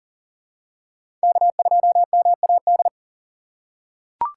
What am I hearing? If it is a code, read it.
Morse code, heard as a keyed tone, about 40 wpm, 700 Hz. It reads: K2MAD
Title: USA Call Signs - 5 Characters - 3x Character Spacing - 40wpm